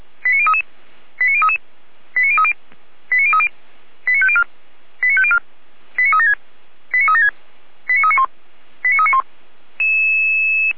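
Five-tone selective call (ZVEI), heard as quick runs of five tones, tones of about 70 ms. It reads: 89033 89033 89030 89030 89504 89504 89376 89376 89391 89391
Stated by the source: ZVEI five-tone selective call received on SDR.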